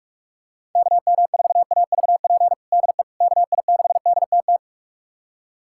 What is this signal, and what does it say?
Morse code, heard as a keyed tone, about 42 wpm, 700 Hz. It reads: KM4AVP DE KI6DTT
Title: USA Callsign DE Callsign - 6 Characters - 45wpm